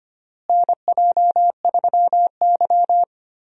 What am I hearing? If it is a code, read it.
Morse code, heard as a keyed tone, about 25 wpm, 700 Hz. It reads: NJ3Y
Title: USA Call Signs - 4 Characters - Repeated 3x - 25wpm